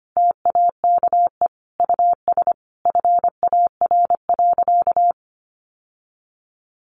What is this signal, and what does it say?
Morse code, heard as a keyed tone, about 25 wpm, 700 Hz. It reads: TAKE US FAR.
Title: Sentences from Top 1500 Words 25wpm